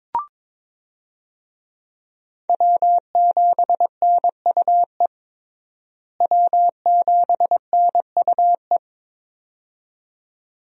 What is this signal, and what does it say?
Morse code, heard as a keyed tone, about 21 wpm, 700 Hz. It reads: W7NUE W7NUE
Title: USA Call Signs - 5 Characters - Repeated 2x - 22wpm